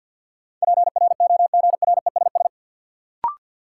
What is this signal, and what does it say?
Morse code, heard as a keyed tone, about 53 wpm, 700 Hz. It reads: PROGRESS
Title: Sentences from Top 1400 Words - New Word Drill - 50wpm